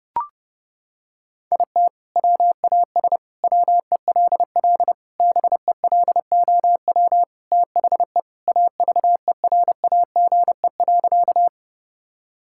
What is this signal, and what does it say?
Morse code, heard as a keyed tone, about 30 wpm, 700 Hz. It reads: IT WAS WELL BELOW THE AVERAGE.
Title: Sentences from Top 1000 Words - Review Part 6 - 30wpm